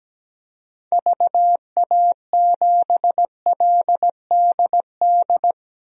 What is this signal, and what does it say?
Morse code, heard as a keyed tone, about 17 wpm, 700 Hz. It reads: VA7LDD
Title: DXCC-100 Call Signs - 6 Characters - 17wpm